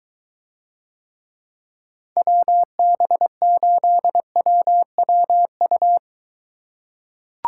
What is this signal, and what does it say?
Morse code, heard as a keyed tone, about 23 wpm, 700 Hz. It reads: WB8WWU